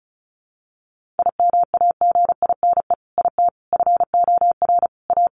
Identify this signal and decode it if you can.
Morse code, heard as a keyed tone, about 34 wpm, 700 Hz. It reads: IMAGINE IT FOR A
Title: Sentences from Top 1400 Words 35wpm